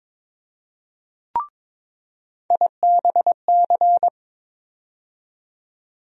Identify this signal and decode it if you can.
Morse code, heard as a keyed tone, about 22 wpm, 700 Hz. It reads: IBC
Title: USA Call Sign Postfixes - 3 Letters 22wpm